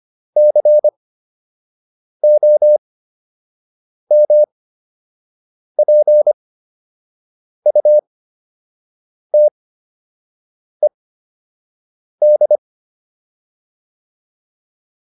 Morse code, heard as a keyed tone, about 25 wpm, spaced as for 6 wpm, 600 Hz. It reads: COMPUTED